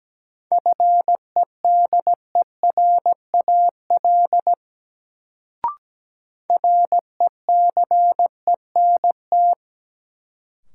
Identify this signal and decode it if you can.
Morse code, heard as a keyed tone, about 17 wpm, 700 Hz. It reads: FEDERAL RECENT
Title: Sets of 3 Words - New Word Drill - 17wpm